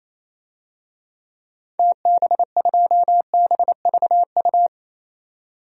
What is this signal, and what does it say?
Morse code, heard as a keyed tone, about 27 wpm, 700 Hz. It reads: TB2BVU
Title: DXCC-100 Call Signs - 6 Characters - 28wpm